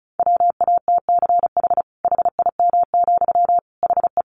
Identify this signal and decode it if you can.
Morse code, heard as a keyed tone, about 34 wpm, 700 Hz. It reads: WATCH HIM, HE
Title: Sentences from Top 300 Words 35wpm